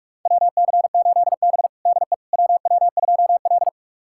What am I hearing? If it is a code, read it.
Morse code, heard as a keyed tone, about 41 wpm, 700 Hz. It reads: WC8B DE WW2L
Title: USA Callsign DE Callsign - 4 Characters - 45wpm